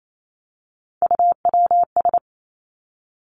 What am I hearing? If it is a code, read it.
Morse code, heard as a keyed tone, about 28 wpm, 700 Hz. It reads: UWS
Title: Three Challenging Letter-Number Combinations 28wpm